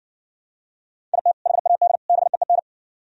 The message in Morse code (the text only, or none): IT HAD BEEN